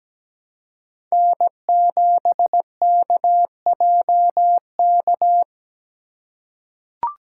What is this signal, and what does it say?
Morse code, heard as a keyed tone, about 17 wpm, 700 Hz. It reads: N7KJK
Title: USA Call Signs - 5 Characters - 4x Character Spacing - 17wpm